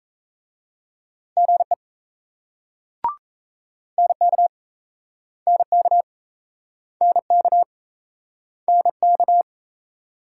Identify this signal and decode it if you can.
Morse code, heard as a keyed tone, about 38 wpm, 700 Hz. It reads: GE NK NK NK NK